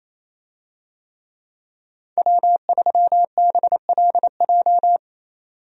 Morse code, heard as a keyed tone, about 28 wpm, 700 Hz. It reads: W3BLJ
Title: USA Call Signs - 5 Characters - 28wpm